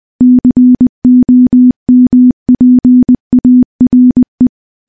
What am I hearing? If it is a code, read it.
Morse code, heard as a keyed tone, about 20 wpm, 250 Hz. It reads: COMPARE
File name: morse-training-20WPM